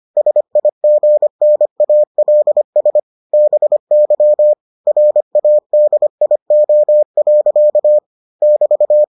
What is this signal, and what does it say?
Morse code, heard as a keyed tone, about 25 wpm, 600 Hz. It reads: SIGNALS BY RADIO. <BT>